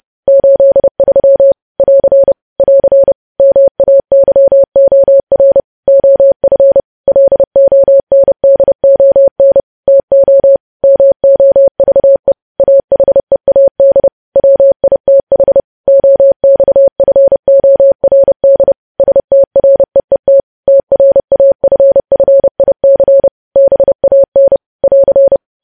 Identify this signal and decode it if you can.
Morse code, heard as a keyed tone, about 30 wpm, 550 Hz. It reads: 83 <AR> <AR> MAYOR OF LONDON TO MOVE AHEAD WITH OXFORD STREET TRAFFIC BAN <AR>
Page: news headlines in morse code at 30 wpm